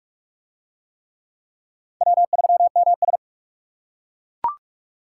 Morse code, heard as a keyed tone, about 41 wpm, 700 Hz. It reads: W3MS